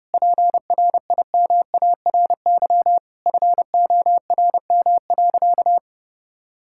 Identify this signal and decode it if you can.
Morse code, heard as a keyed tone, about 30 wpm, 700 Hz. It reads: PRIMARY FORM.
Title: Sentences from Top 1000 Words - Review Part 5 - 30wpm